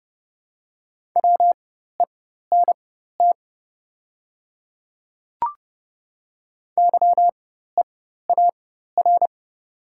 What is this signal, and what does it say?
Morse code, heard as a keyed tone, about 30 wpm, spaced as for 14 wpm, 690 Hz. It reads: WENT YEAR
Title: 4x Character Spacing - Top 200 Words - 30wpm